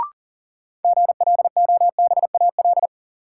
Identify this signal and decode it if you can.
Morse code, heard as a keyed tone, about 38 wpm, 700 Hz. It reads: GLOBAL
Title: Sentences from Top 1200 Words - New Word Drill Alt - 40wpm